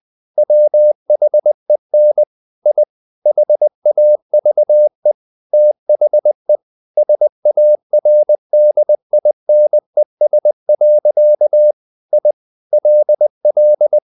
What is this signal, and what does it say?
Morse code, heard as a keyed tone, about 20 wpm, 600 Hz. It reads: WHEN I HAVE THE SARDINES. I LL
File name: the old man and the sea / the old man and the sea part 1 20wpm